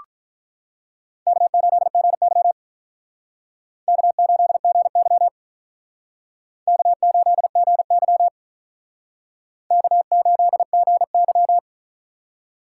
Morse code, heard as a keyed tone, about 48 wpm, 700 Hz. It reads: K8GY K8GY K8GY K8GY